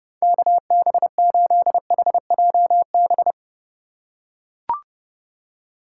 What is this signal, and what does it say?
Morse code, heard as a keyed tone, about 32 wpm, 700 Hz. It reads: KB8HJB
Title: USA Call Signs - 6 Characters - 2x Character Spacing - 30wpm